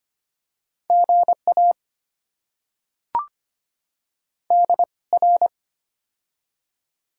Morse code, heard as a keyed tone, about 25 wpm, spaced as for 21 wpm, 700 Hz. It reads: GA DR
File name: Top 90 QSO Elements - 2x Character Spacing - 25wpm